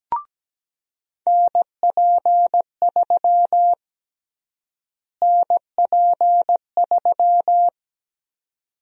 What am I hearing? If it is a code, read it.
Morse code, heard as a keyed tone, about 17 wpm, 700 Hz. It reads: NP3 NP3
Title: USA Call Sign Prefixes - Repeated 2x - 17wpm